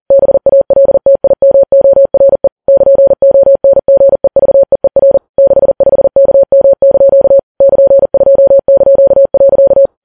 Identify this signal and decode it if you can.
Morse code, heard as a keyed tone, about 40 wpm, 550 Hz. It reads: BALTIMORE <KN>ONGEVEER 65KM) <KN>2).
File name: Files Web tls / Morse on9fv / morse40kps